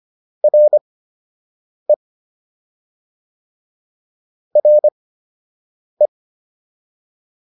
Morse code, heard as a keyed tone, about 25 wpm, spaced as for 7 wpm, 600 Hz. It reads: RE RE